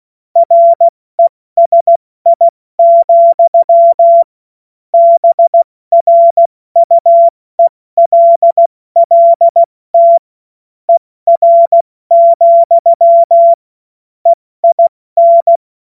Morse code, heard as a keyed tone, about 16 wpm, 680 Hz. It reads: RESI, BRUELLT ER, EIN
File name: Morsekurs / Witze / Witz 13 Mai 2025 70 F80